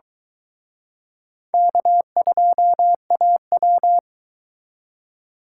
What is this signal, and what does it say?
Morse code, heard as a keyed tone, about 23 wpm, 700 Hz. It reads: K2AW